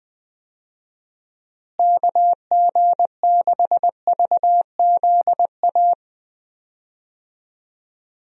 Morse code, heard as a keyed tone, about 20 wpm, 700 Hz. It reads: KG6VZA
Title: USA Call Signs - 6 Characters - Repeated 3x - 20wpm